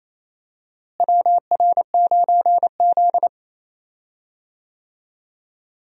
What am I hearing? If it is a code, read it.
Morse code, heard as a keyed tone, about 27 wpm, 700 Hz. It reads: WR9Z
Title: USA Call Signs - 4 Characters - Repeated 2x - 28wpm